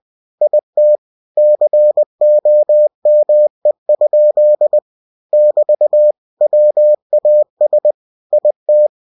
Morse code, heard as a keyed tone, about 20 wpm, 600 Hz. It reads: IT COME? <BT> WAS IT
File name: cw-books / winnie-the-pooh / winnie-20wpm-page-0041